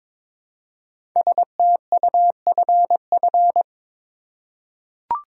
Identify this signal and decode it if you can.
Morse code, heard as a keyed tone, about 22 wpm, 700 Hz. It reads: STUFF
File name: 5 Letter Words - First 3 Letters Given - 22wpm